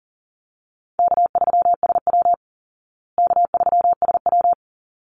Morse code, heard as a keyed tone, about 40 wpm, 700 Hz. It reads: K3SW K3SW
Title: Easy Call Signs Repeated 2x - 40wpm